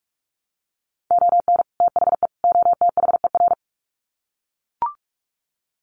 Morse code, heard as a keyed tone, about 44 wpm, 700 Hz. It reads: ON THE OTHER
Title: Sets of 3 Words 45wpm